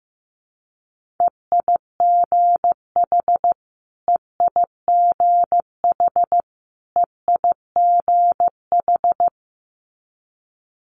Morse code, heard as a keyed tone, about 15 wpm, 700 Hz. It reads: EIGH EIGH EIGH
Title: Multi-letter Phonemes Repeated 3x 15wpm